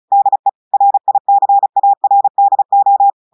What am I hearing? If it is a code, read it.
Morse code, heard as a keyed tone, about 36 wpm, 800 Hz. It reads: DE RICARDO
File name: audios / portugais / portugais texte en clair 1 vitesse 35 mots minutes